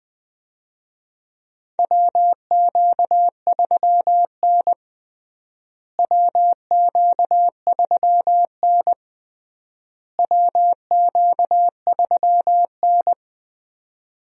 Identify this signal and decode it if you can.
Morse code, heard as a keyed tone, about 19 wpm, 700 Hz. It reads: WQ3N WQ3N WQ3N
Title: USA Call Signs - 4 Characters - Repeated 3x - 20wpm